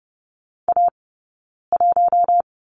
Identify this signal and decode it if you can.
Morse code, heard as a keyed tone, about 30 wpm, 700 Hz. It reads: A 1
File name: Lesson 20d - Number 3 - Two Characters 30wpm